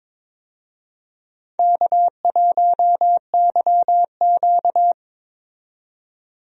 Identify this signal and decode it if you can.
Morse code, heard as a keyed tone, about 22 wpm, 700 Hz. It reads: K1YQ
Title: USA Call Signs - 4 Characters - 22wpm